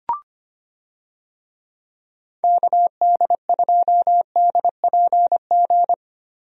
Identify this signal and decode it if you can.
Morse code, heard as a keyed tone, about 25 wpm, 700 Hz. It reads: KD2DPG
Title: USA Call Signs - 6 Characters - 25wpm